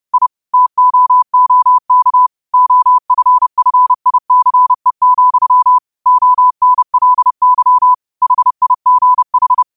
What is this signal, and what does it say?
Morse code, heard as a keyed tone, about 30 wpm, 1000 Hz. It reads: I TOOK OFFICE, ONLY HIGH
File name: qotdfiles / 30wpm